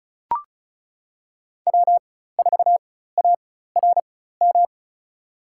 Morse code, heard as a keyed tone, about 36 wpm, spaced as for 16 wpm, 700 Hz. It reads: W4ARM
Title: USA Call Signs - 5 Characters - 4x Character Spacing - 35wpm